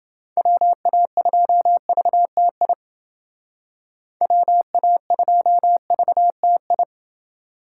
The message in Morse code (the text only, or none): WA2VTI WA2VTI